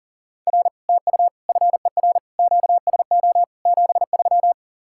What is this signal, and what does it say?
Morse code, heard as a keyed tone, about 52 wpm, 700 Hz. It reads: R TU FER QSO 73